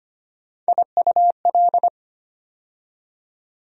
Morse code, heard as a keyed tone, about 25 wpm, 700 Hz. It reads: IUL
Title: USA Call Sign Postfixes - 3 Letters 25wpm